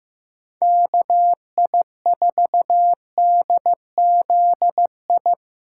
Morse code, heard as a keyed tone, about 15 wpm, 700 Hz. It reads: KI4DZI
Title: USA Call Signs - 6 Characters - 3x Character Spacing - 15wpm